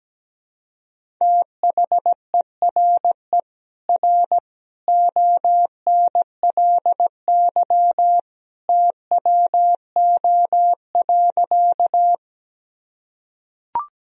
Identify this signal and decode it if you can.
Morse code, heard as a keyed tone, about 17 wpm, 700 Hz. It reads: THERE R ONLY TWO.
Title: Sentences from Top 200 Words and Abbreviations 17wpm